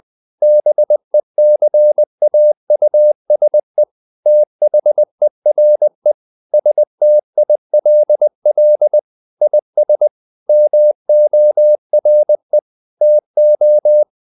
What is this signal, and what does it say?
Morse code, heard as a keyed tone, about 20 wpm, 600 Hz. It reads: BECAUSE THERE STILL IS MORE TO
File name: the old man and the sea part 2 20wpm